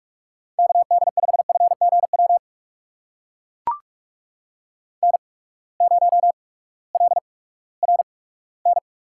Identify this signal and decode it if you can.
Morse code, heard as a keyed tone, about 44 wpm, 690 Hz. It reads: K D 5 F G W N 0 L R N
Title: USA Call Signs - 8x Character Spacing - 45wpm